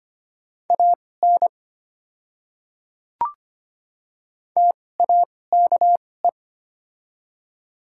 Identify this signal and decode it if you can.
Morse code, heard as a keyed tone, about 25 wpm, spaced as for 18 wpm, 690 Hz. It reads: AN TAKE
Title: 2x Character Spacing - Top 200 Words - 25wpm